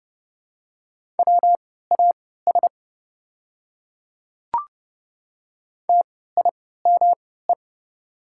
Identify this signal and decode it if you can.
Morse code, heard as a keyed tone, about 30 wpm, spaced as for 17 wpm, 700 Hz. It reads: WAS TIME
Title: 3x Character Spacing - Top 100 Words - 30wpm